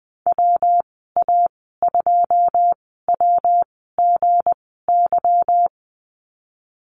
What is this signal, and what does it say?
Morse code, heard as a keyed tone, about 20 wpm, spaced as for 14 wpm, 700 Hz. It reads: WA2WGY
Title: 179 Lesson 29g - Slash - Call Signs 20wpm